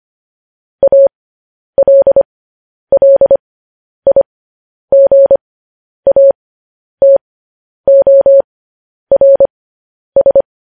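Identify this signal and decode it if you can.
Morse code, heard as a keyed tone, about 25 wpm, spaced as for 10 wpm, 550 Hz. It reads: ALLIGATORS
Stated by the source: Morse code news headlines practice transmission.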